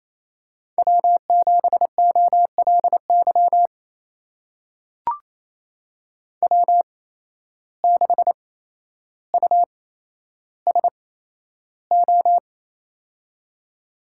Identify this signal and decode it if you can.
Morse code, heard as a keyed tone, about 28 wpm, spaced as for 12 wpm, 700 Hz. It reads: W7OLY W6USO
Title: USA Call Signs - 5 Characters - 8x Character Spacing - 28wpm